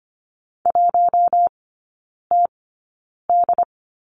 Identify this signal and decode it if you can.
Morse code, heard as a keyed tone, about 25 wpm, spaced as for 9 wpm, 700 Hz. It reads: 1TD
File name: Lesson 14e - Number 2 - Three Characters 25wpm